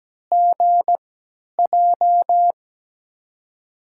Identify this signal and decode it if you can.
Morse code, heard as a keyed tone, about 17 wpm, spaced as for 10 wpm, 700 Hz. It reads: GJ
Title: USA Call Signs - 6 Characters - 3x Character Spacing - 17wpm